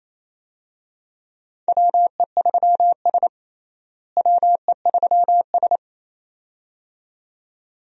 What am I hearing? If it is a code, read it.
Morse code, heard as a keyed tone, about 28 wpm, 700 Hz. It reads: WE3S WE3S
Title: USA Call Signs - 4 Characters - Repeated 2x - 28wpm